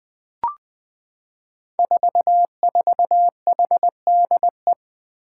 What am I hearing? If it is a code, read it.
Morse code, heard as a keyed tone, about 20 wpm, 700 Hz. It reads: 44HDE
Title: Five Challenging Letter-Number Combinations 20wpm